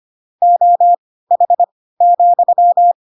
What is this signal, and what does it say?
Morse code, heard as a keyed tone, about 25 wpm, spaced as for 16 wpm, 700 Hz. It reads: OH,
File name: Pinocchio / Pinocchio ch7 10 25 ik1hge